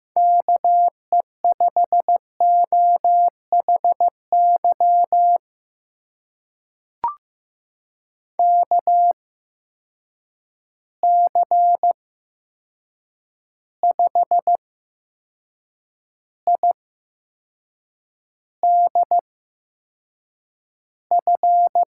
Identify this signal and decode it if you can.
Morse code, heard as a keyed tone, about 15 wpm, spaced as for 6 wpm, 700 Hz. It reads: KE5OHY KC5IDF
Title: USA Call Signs - 6 Characters - 8x Character Spacing - 15wpm